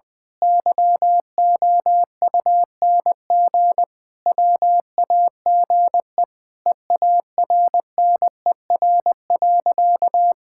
Morse code, heard as a keyed tone, about 21 wpm, 700 Hz. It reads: YOUNG WAGE EARNER.
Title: Pangram Challenge v2 - 20wpm